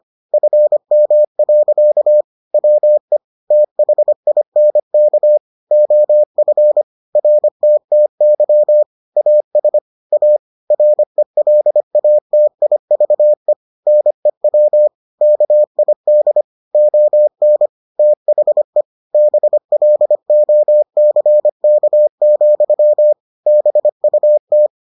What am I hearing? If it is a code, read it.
Morse code, heard as a keyed tone, about 25 wpm, 600 Hz. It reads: FM. WE THINK OF RTTY AS A RELATIVE NEW KID ON THE BLOCK, BUT